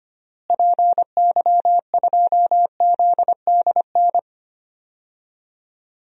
Morse code, heard as a keyed tone, about 25 wpm, 700 Hz. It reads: PY2ZDN
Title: DXCC-100 Call Signs - 6 Characters - Repeated 3x - 25wpm